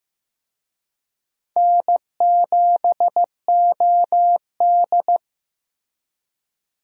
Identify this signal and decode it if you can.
Morse code, heard as a keyed tone, about 15 wpm, 700 Hz. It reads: N7OD
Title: USA Call Signs - 4 Characters - Repeated 3x - 15wpm